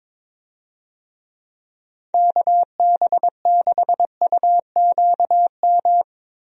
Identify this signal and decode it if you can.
Morse code, heard as a keyed tone, about 21 wpm, 700 Hz. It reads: KB6UQM
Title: USA Call Signs - 6 Characters - Repeated 2x - 22wpm